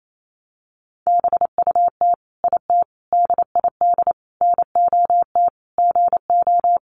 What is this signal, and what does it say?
Morse code, heard as a keyed tone, about 28 wpm, 700 Hz. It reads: BUT IT DID NOT GO
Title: Sentences from Top 600 Words 28wpm